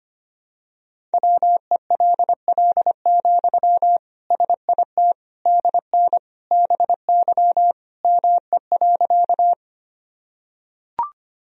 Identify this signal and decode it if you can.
Morse code, heard as a keyed tone, about 25 wpm, 700 Hz. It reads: WELL, SIT DN BY ME.
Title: Sentences from Top 500 Words and Abbreviations 25wpm